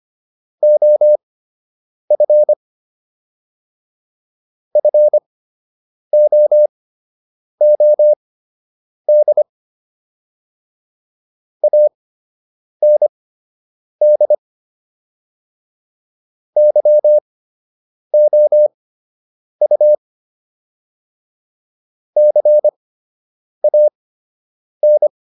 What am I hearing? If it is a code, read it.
Morse code, heard as a keyed tone, about 25 wpm, spaced as for 8 wpm, 600 Hz. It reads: OF FOOD AND YOU CAN